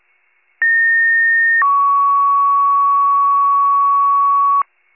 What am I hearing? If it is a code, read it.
Two-tone page, heard as a one-second tone, then a three-second tone, about 1850 then 1120 Hz. It is heard from 0.6 to 4.6 s.
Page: County Wide Paging Archive